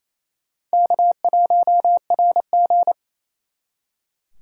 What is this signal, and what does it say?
Morse code, heard as a keyed tone, about 37 wpm, 700 Hz. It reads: K1RG